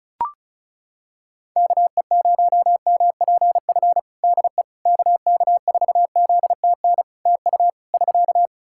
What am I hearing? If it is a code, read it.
Morse code, heard as a keyed tone, about 36 wpm, 700 Hz. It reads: KE0MPF DE KK4ZTN TU <SK>
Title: Practice QSO - Exchange 4 - Ending - 35wpm